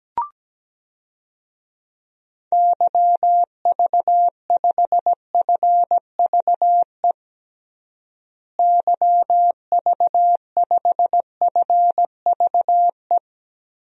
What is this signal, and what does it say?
Morse code, heard as a keyed tone, about 17 wpm, 700 Hz. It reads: YV5FVE YV5FVE
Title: DXCC-100 Call Signs - 6 Characters - Repeated 3x - 17wpm